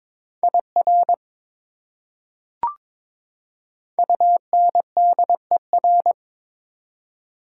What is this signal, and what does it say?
Morse code, heard as a keyed tone, about 21 wpm, 700 Hz. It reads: IR UNDER
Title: Prefixes and Suffixes 22wpm